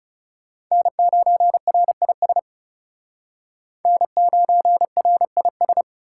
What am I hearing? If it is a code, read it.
Morse code, heard as a keyed tone, about 37 wpm, 700 Hz. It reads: N9RIS N9RIS
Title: Speed Racing - USA Call Signs 30wpm